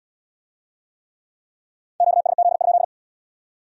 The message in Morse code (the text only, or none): QSZ?